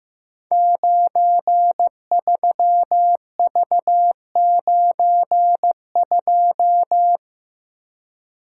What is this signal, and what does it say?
Morse code, heard as a keyed tone, about 15 wpm, 700 Hz. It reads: 93V92
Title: Code Groups - 5 Letter:Numbers - 15wpm